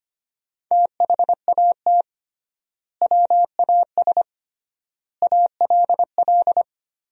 Morse code, heard as a keyed tone, about 25 wpm, 700 Hz. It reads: THAT WAS ALL
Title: 3x Word-Spacing - Sentences from Top 100 Words - 25wpm